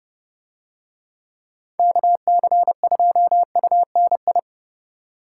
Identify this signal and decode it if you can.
Morse code, heard as a keyed tone, about 28 wpm, 700 Hz. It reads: KC2UNI